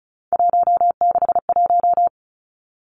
Morse code, heard as a keyed tone, about 36 wpm, 700 Hz. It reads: 161
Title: Three Challenging Letter-Number Combinations 35wpm